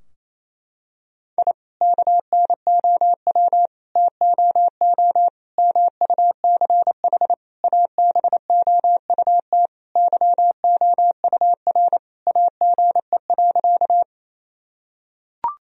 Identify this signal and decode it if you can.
Morse code, heard as a keyed tone, about 28 wpm, 700 Hz. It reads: I KNOW TOO MUCH ABOUT YOUR AGE.
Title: Sentences from Top 1000 Words - Review Part 3 - 28wpm